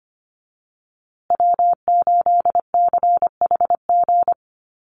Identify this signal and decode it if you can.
Morse code, heard as a keyed tone, about 25 wpm, 700 Hz. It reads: W8CHG
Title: USA Call Signs Repeated 3x 25wpm